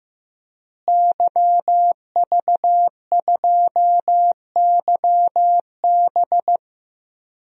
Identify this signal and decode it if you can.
Morse code, heard as a keyed tone, about 15 wpm, 700 Hz. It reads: YV2YB
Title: DXCC-100 Call Signs - 5 Characters - Repeated 3x - 15wpm